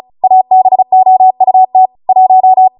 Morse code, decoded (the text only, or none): ABOUT 1